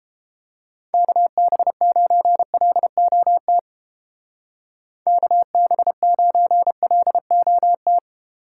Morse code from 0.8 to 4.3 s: KB9LOT

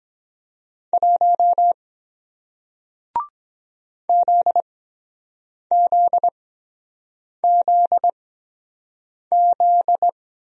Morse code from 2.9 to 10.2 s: ZZZZ

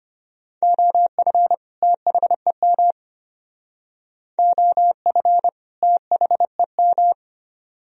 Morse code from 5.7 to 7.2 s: THEM

